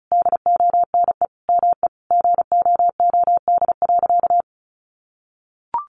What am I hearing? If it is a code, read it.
Morse code, heard as a keyed tone, about 34 wpm, 700 Hz. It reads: DONE ME GOOD.
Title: Sentences from Top 500 Words 35wpm